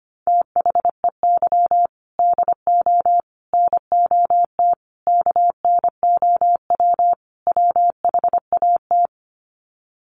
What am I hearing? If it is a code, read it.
Morse code, heard as a keyed tone, about 25 wpm, 700 Hz. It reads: THEY DO NOT KNOW WHAT
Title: Sets of 5 Words Encore 25wpm